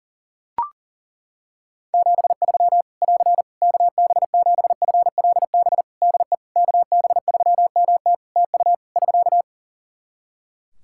Morse code, heard as a keyed tone, about 37 wpm, 700 Hz. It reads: 73 <AR> KB7FLB DE KB3MT TU <SK>
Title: Practice QSO - Exchange 4 - Ending - 40wpm